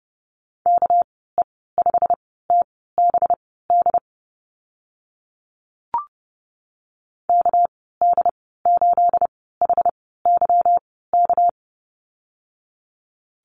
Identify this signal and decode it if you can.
Morse code, heard as a keyed tone, about 30 wpm, spaced as for 17 wpm, 700 Hz. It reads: KE5TBD KD8HYK
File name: Lesson 37g - Letter x - Call Signs 30wpm